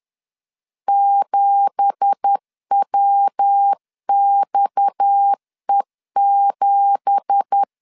Morse code, heard as a keyed tone, about 10 wpm, 800 Hz. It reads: 7WXE7